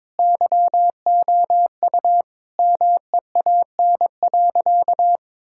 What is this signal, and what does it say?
Morse code, heard as a keyed tone, about 22 wpm, 700 Hz. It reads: YOU MEAN.
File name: practice / 22wpm / Sentences with 2-Word Phrases - 22wpm